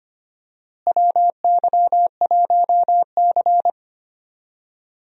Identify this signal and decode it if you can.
Morse code, heard as a keyed tone, about 25 wpm, 700 Hz. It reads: WY1C
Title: USA Call Signs - 4 Characters - Repeated 3x - 25wpm